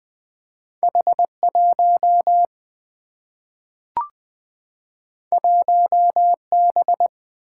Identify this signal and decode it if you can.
Morse code, heard as a keyed tone, about 20 wpm, 700 Hz. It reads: H1 1B
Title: Two Letter-Number Combinations - 20wpm